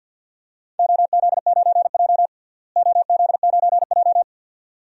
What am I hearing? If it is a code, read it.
Morse code, heard as a keyed tone, about 47 wpm, 700 Hz. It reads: OZ9J OZ9J
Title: DXCC-100 Call Signs - 4 Characters - Repeated 2x - 50wpm